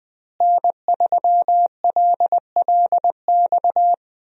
Morse code, heard as a keyed tone, about 20 wpm, 700 Hz. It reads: N3LLX
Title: USA Call Signs - 5 Characters - Repeated 3x - 20wpm